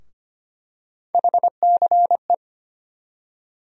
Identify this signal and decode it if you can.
Morse code, heard as a keyed tone, about 25 wpm, 700 Hz. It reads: HCE
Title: USA Call Sign Postfixes - 3 Letters 25wpm